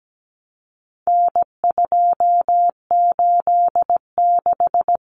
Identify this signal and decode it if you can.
Morse code, heard as a keyed tone, about 17 wpm, 700 Hz. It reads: N286
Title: Common Cut Numbers - 4-Digit 17wpm